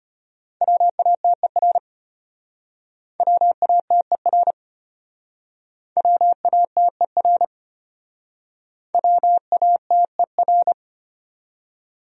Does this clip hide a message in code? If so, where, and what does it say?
Yes: Morse code from 0.6 to 10.7 s, heard as a keyed tone, about 32 wpm, 700 Hz. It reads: WATER WATER WATER WATER